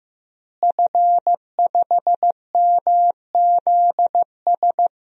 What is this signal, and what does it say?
Morse code, heard as a keyed tone, about 15 wpm, 700 Hz. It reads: F5MZS
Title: DXCC-100 Call Signs - 5 Characters - Repeated 3x - 15wpm